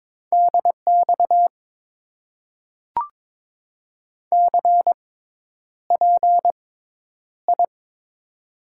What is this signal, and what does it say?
Morse code, heard as a keyed tone, about 22 wpm, spaced as for 10 wpm, 700 Hz. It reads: DX CPI
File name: Top 90 QSO Elements - 6x Character Spacing - 22wpm